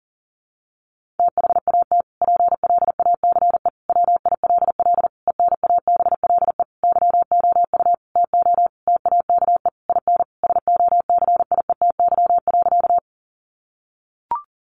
Morse code, heard as a keyed tone, about 40 wpm, 700 Hz. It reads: THAT PLACE WILL ENABLE YOU TO TAKE IN SOCIETY.
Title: Sentences from Top 1500 Words 40wpm